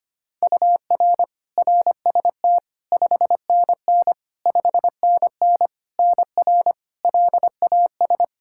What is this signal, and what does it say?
Morse code, heard as a keyed tone, about 25 wpm, 700 Hz. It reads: UR RST 5NN 5NN NR LAS
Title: K1USN SST QSO Practice 25wpm